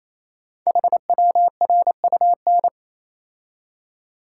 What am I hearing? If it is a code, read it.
Morse code, heard as a keyed tone, about 27 wpm, 700 Hz. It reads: HWRUN
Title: Code Groups - 5 Letters - 28wpm